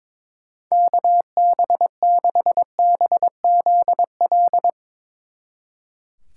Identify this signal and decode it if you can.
Morse code, heard as a keyed tone, about 22 wpm, 700 Hz. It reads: KB6BZL
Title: Speed Racing - USA Call Signs 22wpm